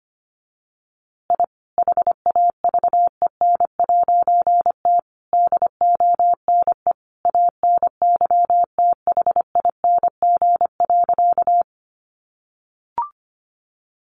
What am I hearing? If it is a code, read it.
Morse code, heard as a keyed tone, about 25 wpm, 700 Hz. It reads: I HAVEN'T DONE ANYTHING.
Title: A Sound Of Thunder 25wpm